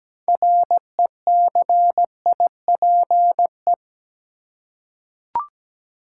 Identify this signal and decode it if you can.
Morse code, heard as a keyed tone, about 17 wpm, 700 Hz. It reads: RECIPE
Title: Six Letter Words - 17wpm